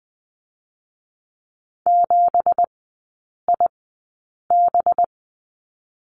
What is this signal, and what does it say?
Morse code, heard as a keyed tone, about 20 wpm, spaced as for 8 wpm, 690 Hz. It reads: 7IB
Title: 205 Lesson 33e - Letter j - Three Characters 20wpm